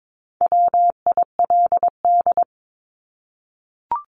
Morse code, heard as a keyed tone, about 22 wpm, 700 Hz. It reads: WILD
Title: Top 1501-1750 Words 22wpm